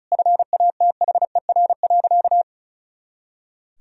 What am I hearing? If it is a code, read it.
Morse code, heard as a keyed tone, about 36 wpm, 700 Hz. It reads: FATHER.
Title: Sentences from Top 300 Words and Abbreviations 35wpm